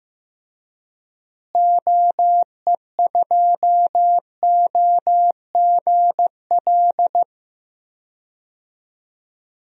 Morse code, heard as a keyed tone, about 15 wpm, 700 Hz. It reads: OE2OGL
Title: DXCC-100 Call Signs - 6 Characters - Repeated 3x - 15wpm